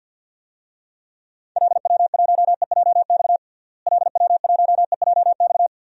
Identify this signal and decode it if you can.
Morse code, heard as a keyed tone, about 53 wpm, 700 Hz. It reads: LW1EJX LW1EJX
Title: DXCC-100 Call Signs - 6 Characters - Repeated 2x - 50wpm